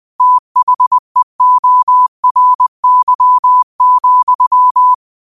Morse code, heard as a keyed tone, about 20 wpm, 1000 Hz. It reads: THEORY,